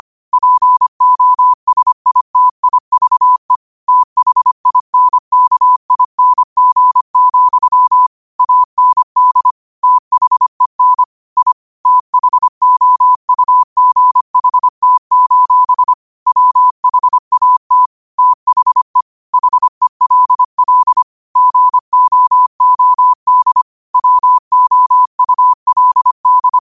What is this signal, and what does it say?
Morse code, heard as a keyed tone, about 25 wpm, 1000 Hz. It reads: POSITIVE THINKING, AND THEN I THOUGHT: WHAT THE HELL GOOD WOULD
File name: qotdfiles / 25wpm